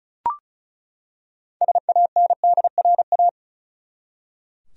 Morse code, heard as a keyed tone, about 36 wpm, 700 Hz. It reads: SANDRA
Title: USA Common Names - 35wpm